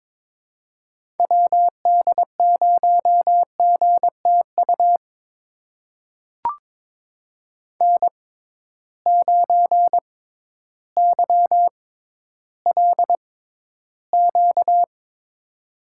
Morse code, heard as a keyed tone, about 22 wpm, spaced as for 12 wpm, 700 Hz. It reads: WD0GTU N9YLQ
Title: USA Call Signs - 6x Character Spacing - 22wpm